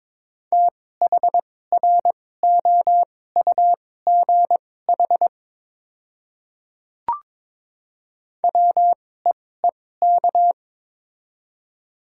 Morse code, heard as a keyed tone, about 22 wpm, spaced as for 16 wpm, 700 Hz. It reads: THROUGH WEEK
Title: 2x Character Spacing - Top 500 Words - 22wpm